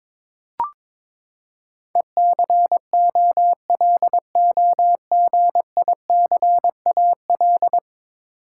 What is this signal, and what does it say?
Morse code, heard as a keyed tone, about 22 wpm, 700 Hz. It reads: ECOLOGICAL
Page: morse-code-ninja-practice-22wpm